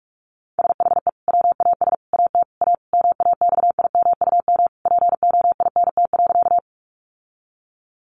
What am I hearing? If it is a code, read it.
Morse code, heard as a keyed tone, about 44 wpm, 700 Hz. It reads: SHE WAS AT A MAXIMUM POINT.
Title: Sentences from Top 1300 Words 45wpm